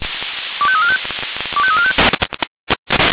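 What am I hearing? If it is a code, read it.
Five-tone selective call (ZVEI), heard as quick runs of five tones, tones of about 70 ms. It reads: 25345 25345
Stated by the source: ZVEI five-tone call received on SDR.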